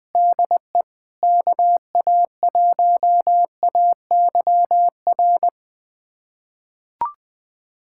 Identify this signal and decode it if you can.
Morse code, heard as a keyed tone, about 21 wpm, 700 Hz. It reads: DE KA1AYR
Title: USA Callsign DE Callsign - 6 Characters - 20wpm